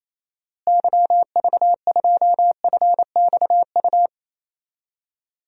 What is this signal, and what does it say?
Morse code, heard as a keyed tone, about 28 wpm, 700 Hz. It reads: YV2FXU
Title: DXCC-100 Call Signs - 6 Characters - Repeated 2x - 28wpm